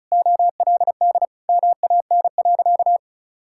Morse code, heard as a keyed tone, about 36 wpm, 700 Hz. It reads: OLD MAN.
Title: Sentences from Top 1600 Words 35wpm